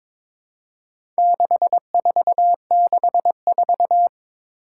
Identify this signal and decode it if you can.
Morse code, heard as a keyed tone, about 22 wpm, 700 Hz. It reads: 6464
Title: Four Challenging Letter-Number Combinations 22wpm